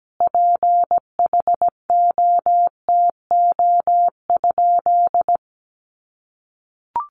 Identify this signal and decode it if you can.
Morse code, heard as a keyed tone, about 17 wpm, 700 Hz. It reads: PHOTO?
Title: Sentences from Top 1200 Words 17wpm